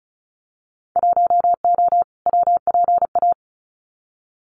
Morse code, heard as a keyed tone, about 34 wpm, 700 Hz. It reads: 1O WPA
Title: Winter Field Day Exchanges 35wpm